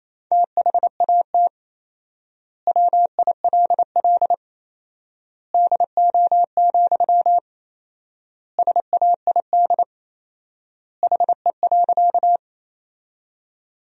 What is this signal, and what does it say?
Morse code, heard as a keyed tone, about 28 wpm, 700 Hz. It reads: THAT WILL DO, SAID HE.
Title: 4x Word-Spacing - Sentences from Top 100 Words - 28wpm